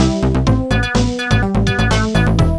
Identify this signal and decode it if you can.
There is no code.